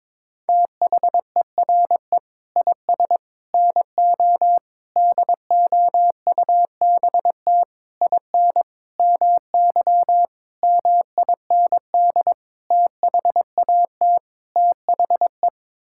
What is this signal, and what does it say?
Morse code, heard as a keyed tone, about 22 wpm, 700 Hz. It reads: THERE IS NO DOUBT IN MY MIND THAT THE
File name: Sets of 9 Words - 22wpm